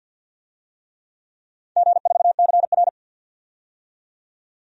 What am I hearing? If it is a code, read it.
Morse code, heard as a keyed tone, about 52 wpm, 700 Hz. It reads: G4CR
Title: DXCC-100 Call Signs - 4 Characters - 50wpm